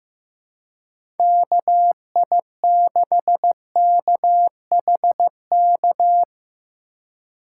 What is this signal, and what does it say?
Morse code, heard as a keyed tone, about 15 wpm, 700 Hz. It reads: KI6KHK
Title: USA Call Signs - 6 Characters - 4x Character Spacing - 15wpm